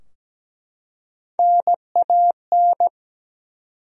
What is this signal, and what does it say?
Morse code, heard as a keyed tone, about 17 wpm, 700 Hz. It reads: NAN
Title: USA Common Names - Max 3 Letters - 17wpm